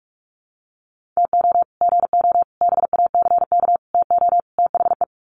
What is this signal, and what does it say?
Morse code, heard as a keyed tone, about 44 wpm, 700 Hz. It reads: TO GO BACK TO THE
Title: Sets of 5 Words 45wpm